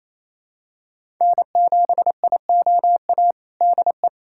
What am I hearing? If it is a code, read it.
Morse code, heard as a keyed tone, about 28 wpm, 700 Hz. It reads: N7IOA DE
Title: USA Callsign DE Callsign 28wpm